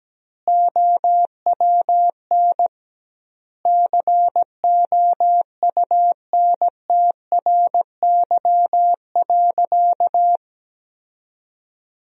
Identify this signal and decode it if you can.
Morse code, heard as a keyed tone, about 17 wpm, 700 Hz. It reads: OWN COUNTRY.
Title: 2x Word-Spacing - Sentences from Top 300 Words - 17wpm